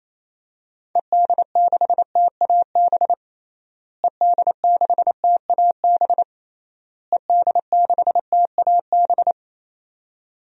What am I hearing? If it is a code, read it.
Morse code, heard as a keyed tone, about 28 wpm, 700 Hz. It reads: ED6TAB ED6TAB ED6TAB
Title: DXCC-100 Call Signs - 6 Characters - Repeated 3x - 28wpm